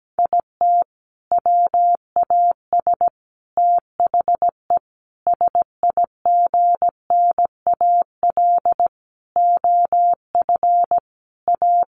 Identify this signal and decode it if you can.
Morse code, heard as a keyed tone, about 17 wpm, 700 Hz. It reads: IT WAS THE SIGNAL OF A
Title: Sentences from Top 1400 Words 17wpm